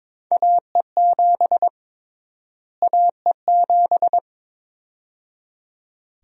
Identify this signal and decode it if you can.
Morse code, heard as a keyed tone, about 22 wpm, 700 Hz. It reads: AE7 AE7
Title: USA Call Sign Prefixes - Repeated 2x - 22wpm